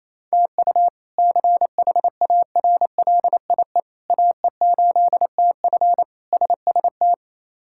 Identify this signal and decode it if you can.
Morse code, heard as a keyed tone, about 28 wpm, 700 Hz. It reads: TU CHARLIE AE8TF SST
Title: K1USN SST QSO Practice - 28wpm